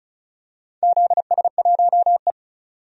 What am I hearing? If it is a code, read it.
Morse code, heard as a keyed tone, about 34 wpm, 700 Hz. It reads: ZS1E
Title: DXCC-100 Call Signs - 4 Characters - 35wpm